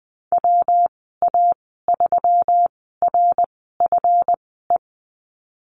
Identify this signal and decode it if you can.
Morse code, heard as a keyed tone, about 20 wpm, spaced as for 15 wpm, 700 Hz. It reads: WA3RFE
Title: 147 Lesson 24g - Letter y - Call Signs 20wpm